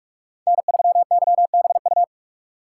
Morse code, heard as a keyed tone, about 45 wpm, 700 Hz. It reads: N3YBU
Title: USA Call Signs - 5 Characters - 3x Character Spacing - 45wpm